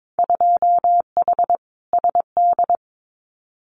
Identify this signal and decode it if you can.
Morse code, heard as a keyed tone, about 22 wpm, 700 Hz. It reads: 2H SD
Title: Winter Field Day Exchanges 22wpm